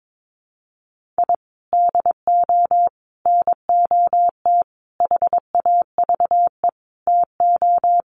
Morse code, heard as a keyed tone, about 22 wpm, 700 Hz. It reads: I DO NOT HAVE TO
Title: Sets of 5 Words 22wpm